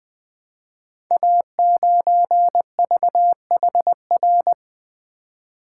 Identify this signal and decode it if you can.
Morse code, heard as a keyed tone, about 20 wpm, 700 Hz. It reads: A9VHR
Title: Code Groups - 5 Letter:Numbers - 20wpm